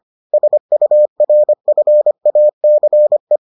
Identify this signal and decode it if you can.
Morse code, heard as a keyed tone, about 25 wpm, 600 Hz. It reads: SURFACE